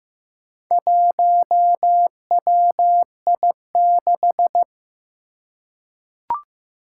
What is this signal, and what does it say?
Morse code, heard as a keyed tone, about 15 wpm, 700 Hz. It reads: 1WI6
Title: Four Challenging Letter-Number Combinations 15wpm